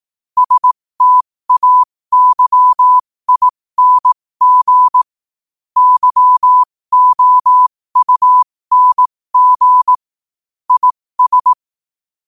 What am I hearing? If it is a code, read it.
Morse code, heard as a keyed tone, about 18 wpm, spaced as for 15 wpm, 1000 Hz. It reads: STAYING YOUNG IS
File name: qotdfiles / 15wpm